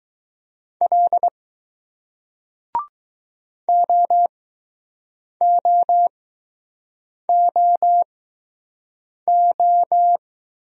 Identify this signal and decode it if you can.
Morse code, heard as a keyed tone, about 20 wpm, spaced as for 6 wpm, 700 Hz. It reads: L OOOO